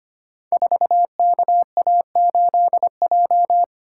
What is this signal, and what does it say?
Morse code, heard as a keyed tone, about 25 wpm, 700 Hz. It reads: 4KA8J
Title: Code Groups - 5 Letter:Numbers - 25wpm